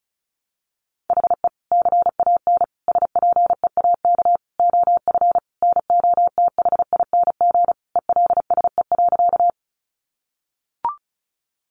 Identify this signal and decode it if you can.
Morse code, heard as a keyed tone, about 34 wpm, 700 Hz. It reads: HE CAN SPEAK OF NOTHING ELSE.
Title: Sentences from Top 1100 Words 35wpm